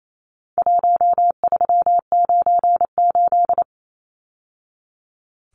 Morse code, heard as a keyed tone, about 28 wpm, 700 Hz. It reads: 1398
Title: Numbers - Contest Style - 4-Digit 28wpm